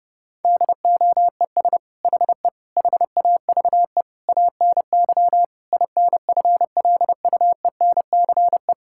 Morse code, heard as a keyed tone, about 30 wpm, 700 Hz. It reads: DOES HE HAVE ANY INFLUENCE
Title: Sentences from Top 1600 Words 30wpm